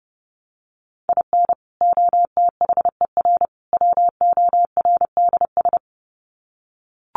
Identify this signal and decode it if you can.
Morse code, heard as a keyed tone, about 30 wpm, 700 Hz. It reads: IN OTHER WORDS
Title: Sets of 3 Words 30wpm